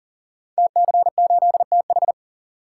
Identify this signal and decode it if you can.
Morse code, heard as a keyed tone, about 39 wpm, 700 Hz. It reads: TC8TH
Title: Code Groups - 5 Letter:Numbers - 40wpm